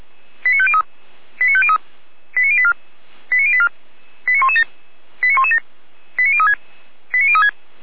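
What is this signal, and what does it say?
Five-tone selective call (ZVEI), heard as quick runs of five tones, tones of about 70 ms. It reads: 89503 89503 89084 89084 89117 89117 89036 89036
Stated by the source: ZVEI five-tone selective call received on SDR.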